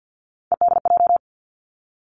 Morse code, heard as a keyed tone, about 50 wpm, 700 Hz. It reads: EDJ
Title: Three Challenging Letter-Number Combinations 50wpm